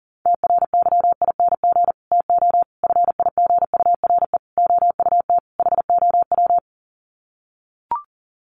Morse code, heard as a keyed tone, about 40 wpm, 700 Hz. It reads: TRYING TO FIGURE OUT HOW
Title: Sets of 5 Words Encore 40wpm